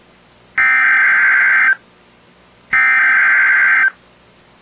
Emergency Alert System (SAME) tones (header bursts), heard as harsh buzzing bursts. It is heard from 0.6 to 3.8 s.